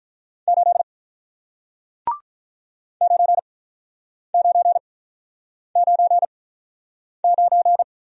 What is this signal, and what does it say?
Morse code, heard as a keyed tone, about 48 wpm, spaced as for 9 wpm, 700 Hz. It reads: 8 9999